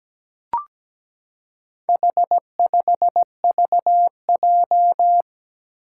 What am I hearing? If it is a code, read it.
Morse code, heard as a keyed tone, about 17 wpm, 700 Hz. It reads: H5VJ
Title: Four Challenging Letter-Number Combinations 17wpm